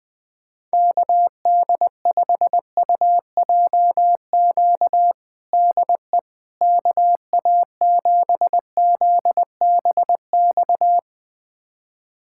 Morse code, heard as a keyed tone, about 21 wpm, 700 Hz. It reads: KD5UJQ DE KA7ZBX
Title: USA Callsign DE Callsign - 6 Characters - 20wpm